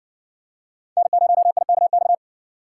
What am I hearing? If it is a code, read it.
Morse code, heard as a keyed tone, about 60 wpm, 700 Hz. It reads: N0IKX